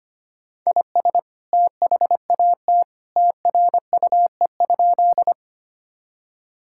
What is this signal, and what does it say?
Morse code, heard as a keyed tone, about 25 wpm, 700 Hz. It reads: IS THAT TRUE?
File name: practice / 25wpm / Sentences with 2-Word Phrases - 25wpm